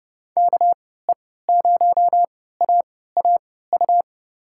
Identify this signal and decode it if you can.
Morse code, heard as a keyed tone, about 31 wpm, spaced as for 17 wpm, 700 Hz. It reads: KE0AAU
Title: USA Call Signs - 6 Characters - 3x Character Spacing - 30wpm